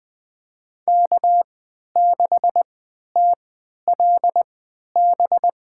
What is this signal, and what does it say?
Morse code, heard as a keyed tone, about 20 wpm, spaced as for 11 wpm, 700 Hz. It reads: K6TLB
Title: USA Call Signs - 5 Characters - 3x Character Spacing - 20wpm